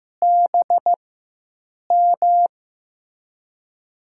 Morse code, heard as a keyed tone, about 15 wpm, spaced as for 7 wpm, 700 Hz. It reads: BM
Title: USA Call Signs - 5 Characters - 4x Character Spacing - 15wpm